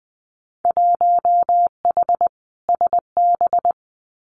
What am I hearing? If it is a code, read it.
Morse code, heard as a keyed tone, about 20 wpm, 700 Hz. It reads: 1H SB
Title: Winter Field Day Exchanges 20wpm